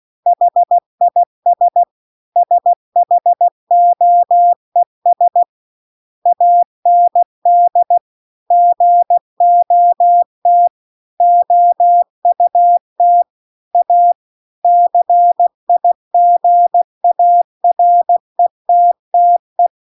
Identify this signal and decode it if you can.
Morse code, heard as a keyed tone, about 16 wpm, 700 Hz. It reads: HIS SHOES AND GOT OUT A CIGARETTE
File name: CW-The-Big-Twoheaded-River-2-16WPM